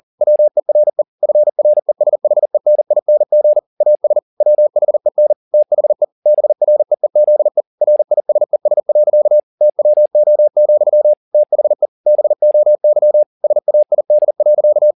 Morse code, heard as a keyed tone, about 40 wpm, 600 Hz. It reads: WERE FRESHENING AS WHEN THE BREEZE RISES. TWO, THE BOY SAID.
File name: the old man and the sea 1 40wpm